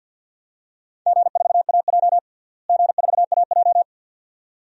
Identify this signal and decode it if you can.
Morse code, heard as a keyed tone, about 52 wpm, 700 Hz. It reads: G4AJ G4AJ
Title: DXCC-100 Call Signs - 4 Characters - Repeated 2x - 50wpm